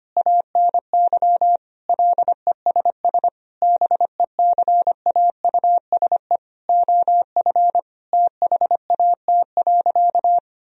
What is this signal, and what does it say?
Morse code, heard as a keyed tone, about 25 wpm, 700 Hz. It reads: ANY LESS BECAUSE OF THAT.
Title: Sentences from Top 1000 Words - Review Part 2 - 25wpm